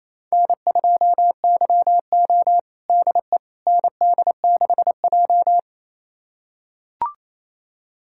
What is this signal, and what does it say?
Morse code, heard as a keyed tone, about 28 wpm, 700 Hz. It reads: N2YO DE ND6J
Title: USA Callsign DE Callsign - 4 Characters - 28wpm